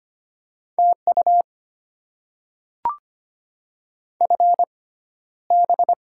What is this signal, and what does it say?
Morse code, heard as a keyed tone, about 25 wpm, spaced as for 13 wpm, 700 Hz. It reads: TU FB
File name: Top 20 QSO Elements - 6x Character Spacing - 25wpm